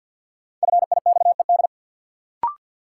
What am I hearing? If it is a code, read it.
Morse code, heard as a keyed tone, about 53 wpm, 700 Hz. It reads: FIXED
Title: Sentences from Top 1100 Words - New Word Drill Alt - 50wpm